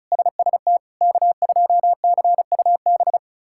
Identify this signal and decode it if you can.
Morse code, heard as a keyed tone, about 36 wpm, 700 Hz. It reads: SST K2CUB
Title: K1USN SST QSO Practice - 35wpm